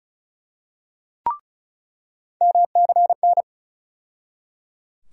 Morse code, heard as a keyed tone, about 36 wpm, 700 Hz. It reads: MCN
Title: USA Call Sign Postfixes - 3 Letters 35wpm